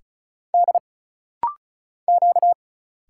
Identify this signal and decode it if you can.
Morse code, heard as a keyed tone, about 34 wpm, 700 Hz. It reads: D Q
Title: Single Letter-Number Alt - Rapid-Fire 35wpm